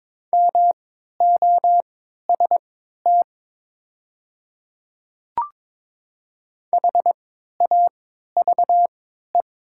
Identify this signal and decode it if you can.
Morse code, heard as a keyed tone, about 21 wpm, spaced as for 12 wpm, 700 Hz. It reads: MOST HAVE
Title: 3x Character Spacing - Top 100 Words - 22wpm